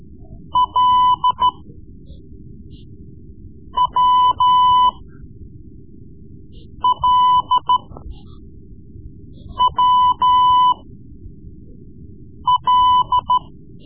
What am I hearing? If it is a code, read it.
Morse code, heard as a keyed tone, about 13 wpm, spaced as for 4 wpm, 1000 Hz. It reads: LWLWL
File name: Morse Lessons / Morse Lessons / Tape-3B